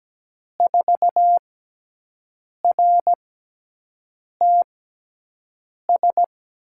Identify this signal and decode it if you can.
Morse code, heard as a keyed tone, about 17 wpm, spaced as for 6 wpm, 700 Hz. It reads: 4RTS
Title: USA Call Signs - 5 Characters - 6x Character Spacing - 17wpm